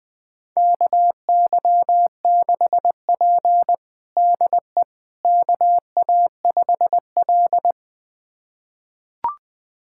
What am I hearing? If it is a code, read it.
Morse code, heard as a keyed tone, about 20 wpm, 700 Hz. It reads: KY6P DE KA5L
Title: USA Callsign DE Callsign - 4 Characters - 20wpm